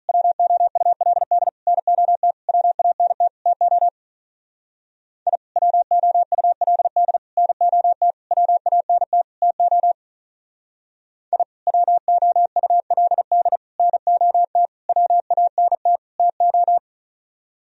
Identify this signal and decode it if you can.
Morse code, heard as a keyed tone, about 43 wpm, 700 Hz. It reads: WOULD NOT WANT TO I WOULD NOT WANT TO I WOULD NOT WANT TO
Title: Speed-Racing - Sets of 5 Words 35wpm